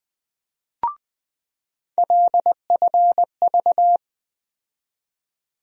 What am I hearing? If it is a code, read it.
Morse code, heard as a keyed tone, about 20 wpm, 700 Hz. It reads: LFV
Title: USA Call Sign Postfixes - 3 Letters 20wpm